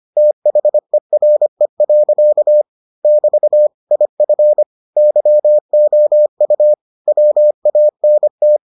Morse code, heard as a keyed tone, about 25 wpm, 600 Hz. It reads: THERE. <BT> IF YOU WANT